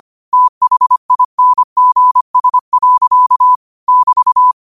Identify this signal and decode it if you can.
Morse code, heard as a keyed tone, about 25 wpm, 1000 Hz. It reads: THINGS. <BT>
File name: qotdfiles / 25wpm